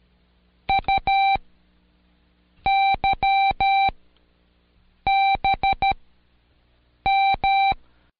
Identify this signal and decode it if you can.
Morse code, heard as a keyed tone, about 13 wpm, 780 Hz. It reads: U Y B M
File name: Code Quick / AudioDownld1 / Seg02-Day1-Intro E